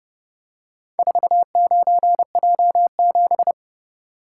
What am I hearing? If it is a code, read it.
Morse code, heard as a keyed tone, about 29 wpm, 700 Hz. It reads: 49J7